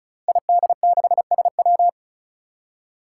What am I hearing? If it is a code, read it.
Morse code, heard as a keyed tone, about 36 wpm, 700 Hz. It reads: ID6SW
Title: Five Challenging Letter-Number Combinations 35wpm